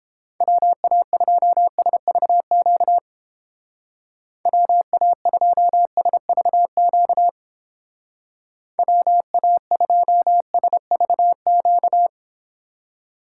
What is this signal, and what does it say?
Morse code, heard as a keyed tone, about 31 wpm, 700 Hz. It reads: WA2SVQ WA2SVQ WA2SVQ